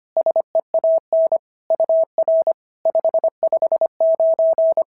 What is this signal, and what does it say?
Morse code, heard as a keyed tone, about 25 wpm, 650 Hz. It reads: SEAN UR 559
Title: SOTA QSO Practice 25wpm